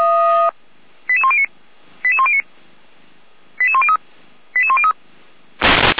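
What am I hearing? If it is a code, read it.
Five-tone selective call (ZVEI), heard as quick runs of five tones, tones of about 80 ms. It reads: 88109 88109 88103 88103
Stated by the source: ZVEI five-tone call received by SDR.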